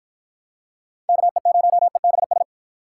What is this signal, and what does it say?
Morse code, heard as a keyed tone, about 53 wpm, 700 Hz. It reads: KE0EBS